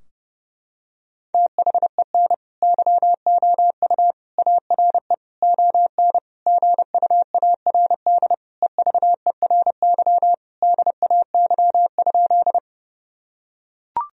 Morse code, heard as a keyed tone, about 28 wpm, 700 Hz. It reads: THEN YOU ARE ON GUARD EVERY DAY?
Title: Sentences from Top 1000 Words - Review Part 4 - 30wpm